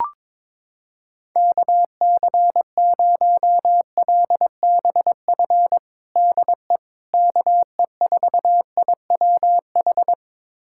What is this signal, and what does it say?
Morse code, heard as a keyed tone, about 21 wpm, 700 Hz. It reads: KC0LBF DE KE4IWH
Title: USA Callsign DE Callsign - 6 Characters - 22wpm